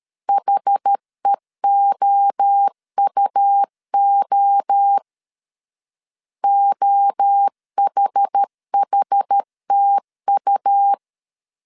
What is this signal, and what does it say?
Morse code, heard as a keyed tone, about 13 wpm, 800 Hz. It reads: HEOUO OHHTU
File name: CW / DET / 1 3 1500C S B 12wpm 2828